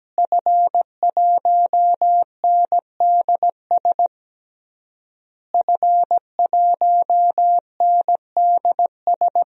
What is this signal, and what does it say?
Morse code, heard as a keyed tone, about 17 wpm, 700 Hz. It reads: F1NDS F1NDS
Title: DXCC-100 Call Signs - 5 Characters - Repeated 3x - 17wpm